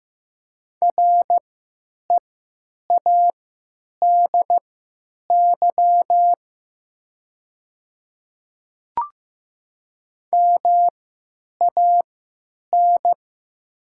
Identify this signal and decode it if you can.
Morse code, heard as a keyed tone, about 15 wpm, spaced as for 9 wpm, 700 Hz. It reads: READY MAN
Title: 3x Character Spacing - Top 500 Words - 15wpm